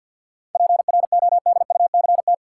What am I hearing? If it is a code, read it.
Morse code, heard as a keyed tone, about 49 wpm, 700 Hz. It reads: PRODUCT
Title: Sentences from Top 1000 Words - Review Part 2 - New Word Drill - 50wpm